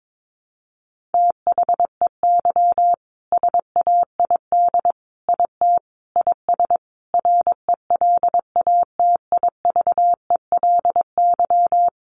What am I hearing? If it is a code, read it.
Morse code, heard as a keyed tone, about 22 wpm, 700 Hz. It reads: THEY SAID IT IS RELATIVELY
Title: Sentences from Top 1100 Words 22wpm